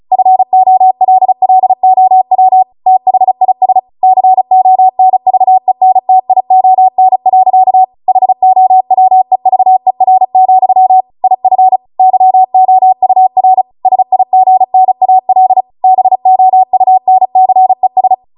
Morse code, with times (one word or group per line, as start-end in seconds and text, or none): 0.1-2.6 FOLLOW
2.9-3.8 THIS
4.0-7.8 CONVENTION.
8.1-11.0 HOWEVER,
11.2-11.8 IF
12.0-13.6 YOUR
13.9-15.6 SIGNAL
15.8-18.1 BOUNCES